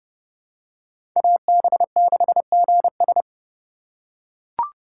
Morse code, heard as a keyed tone, about 30 wpm, 700 Hz. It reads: AB6GS
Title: USA Call Signs - 5 Characters - 3x Character Spacing - 30wpm